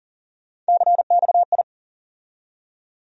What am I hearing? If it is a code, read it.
Morse code, heard as a keyed tone, about 40 wpm, 700 Hz. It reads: CXI
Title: USA Call Sign Postfixes - 3 Letters 40wpm